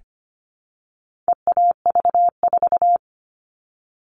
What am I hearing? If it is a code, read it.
Morse code, heard as a keyed tone, about 25 wpm, 700 Hz. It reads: EAV4
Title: All Cut Numbers - 4-Digit 25wpm